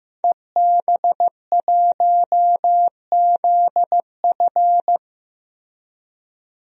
Morse code, heard as a keyed tone, about 15 wpm, 700 Hz. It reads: EB1ZF
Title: DXCC-100 Call Signs - 5 Characters - 15wpm